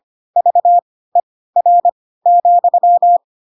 Morse code, heard as a keyed tone, about 25 wpm, spaced as for 16 wpm, 700 Hz. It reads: VER,
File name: Pinocchio / Pinocchio ch4 10 25 ik1hge